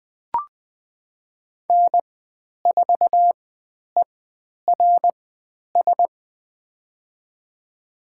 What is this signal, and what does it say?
Morse code, heard as a keyed tone, about 20 wpm, spaced as for 10 wpm, 700 Hz. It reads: N4ERS
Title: 095 Lesson 16g - Letter c - Call Signs 20wpm